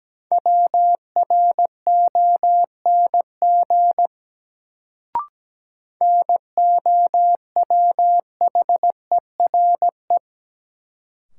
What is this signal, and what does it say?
Morse code, heard as a keyed tone, about 17 wpm, 700 Hz. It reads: WRONG NOWHERE
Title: Sets of 5 Words Encore - New Word Drill - 17wpm